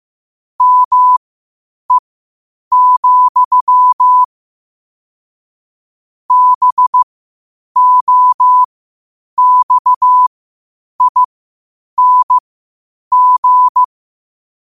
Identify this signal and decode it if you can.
Morse code, heard as a keyed tone, about 15 wpm, spaced as for 8 wpm, 1000 Hz. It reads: ME, BOXING
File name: qotdfiles / 8wpm